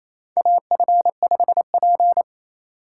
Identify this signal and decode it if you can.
Morse code, heard as a keyed tone, about 27 wpm, 700 Hz. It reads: AF5P
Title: K1USN SST QSO Practice - 28wpm